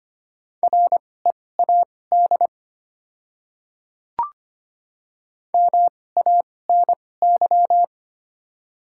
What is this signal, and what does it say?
Morse code, heard as a keyed tone, about 25 wpm, spaced as for 18 wpm, 700 Hz. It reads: READ MANY
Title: 2x Character Spacing - Top 300 Words - 25wpm